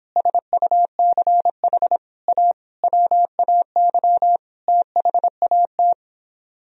Morse code, heard as a keyed tone, about 29 wpm, 700 Hz. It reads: SUCH A WAY THAT